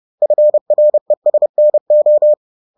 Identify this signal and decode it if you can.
Morse code, heard as a keyed tone, about 30 wpm, 600 Hz. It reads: FRESNO